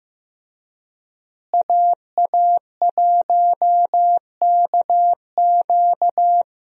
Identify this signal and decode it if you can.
Morse code, heard as a keyed tone, about 15 wpm, 700 Hz. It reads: AA1KQ
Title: USA Call Signs - 5 Characters - Repeated 3x - 15wpm